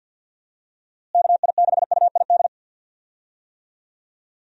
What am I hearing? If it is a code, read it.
Morse code, heard as a keyed tone, about 47 wpm, 700 Hz. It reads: KI6UID
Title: USA Call Signs - 6 Characters - Repeated 2x - 50wpm